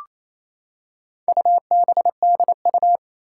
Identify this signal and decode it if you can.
Morse code, heard as a keyed tone, about 27 wpm, 700 Hz. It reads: UBDU
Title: Four Challenging Letter-Number Combinations 28wpm